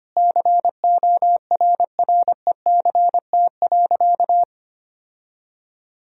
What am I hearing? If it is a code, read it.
Morse code, heard as a keyed tone, about 25 wpm, 700 Hz. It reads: CORRECT.
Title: Sentences from Top 1000 Words - Review Part 2 - 25wpm